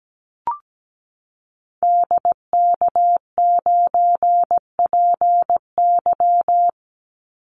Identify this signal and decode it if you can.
Morse code, heard as a keyed tone, about 17 wpm, 700 Hz. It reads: DK9PY
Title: Call Signs Repeated 2x - 17wpm